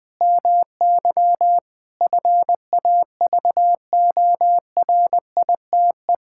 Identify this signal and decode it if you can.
Morse code, heard as a keyed tone, about 20 wpm, 700 Hz. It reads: MY FAVORITE
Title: Sentences with 4-Word Phrases - 20wpm